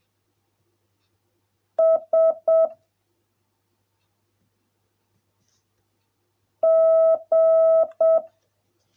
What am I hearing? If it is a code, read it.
Morse code, heard as a keyed tone, about 7 wpm, 650 Hz. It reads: S G